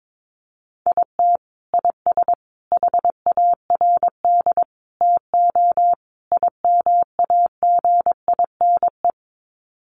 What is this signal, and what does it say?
Morse code, heard as a keyed tone, about 22 wpm, 700 Hz. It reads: IT IS HARD TO IMAGINE
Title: Sets of 5 Words Encore2 22wpm